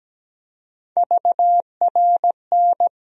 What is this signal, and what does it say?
Morse code, heard as a keyed tone, about 17 wpm, 700 Hz. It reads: VRN
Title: USA Call Sign Postfixes - 3 Letters 17wpm